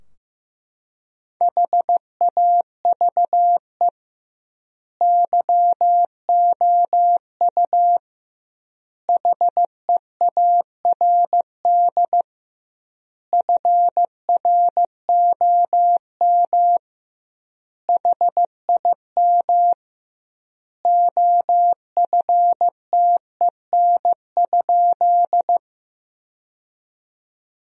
Morse code, heard as a keyed tone, about 15 wpm, 700 Hz. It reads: HAVE YOU HEARD FROM HIM OFTEN?
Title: 2x Word-Spacing - Sentences from Top 400 Words - 15wpm